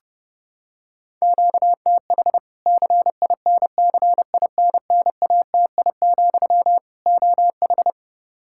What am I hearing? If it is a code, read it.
Morse code, heard as a keyed tone, about 30 wpm, 700 Hz. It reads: QTH CINCINNATI, OH
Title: Practice QSO - Exchange 1 - Beginning - 30wpm